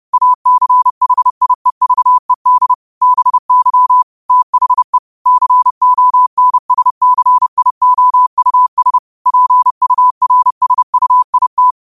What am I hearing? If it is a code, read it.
Morse code, heard as a keyed tone, about 30 wpm, 1000 Hz. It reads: ACHIEVED BY THE CONSCIOUS PURSUIT
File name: qotdfiles / 30wpm